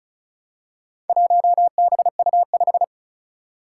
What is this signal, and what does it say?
Morse code, heard as a keyed tone, about 36 wpm, 700 Hz. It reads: 1BU5
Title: Four Challenging Letter-Number Combinations 35wpm